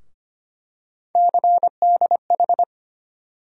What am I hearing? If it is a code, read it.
Morse code, heard as a keyed tone, about 25 wpm, 700 Hz. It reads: CDH
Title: USA Call Sign Postfixes - 3 Letters 25wpm